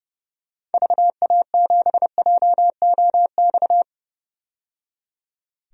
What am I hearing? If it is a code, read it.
Morse code, heard as a keyed tone, about 30 wpm, 700 Hz. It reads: VA7JOX
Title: DXCC-100 Call Signs - 6 Characters - 30wpm